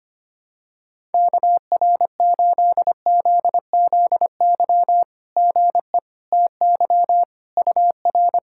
Morse code, heard as a keyed tone, about 25 wpm, 700 Hz. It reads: KR8ZZY GE TY UR
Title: SOTA QSO Practice 25wpm